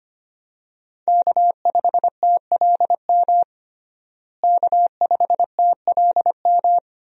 Morse code, heard as a keyed tone, about 25 wpm, 700 Hz. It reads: K5TLM K5TLM
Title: USA Call Signs - 5 Characters - Repeated 3x - 25wpm